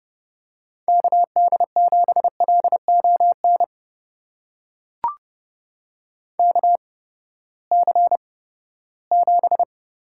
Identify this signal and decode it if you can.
Morse code, heard as a keyed tone, about 32 wpm, spaced as for 17 wpm, 700 Hz. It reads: KD7LON KC7
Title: USA Call Signs - 6 Characters - 8x Character Spacing - 30wpm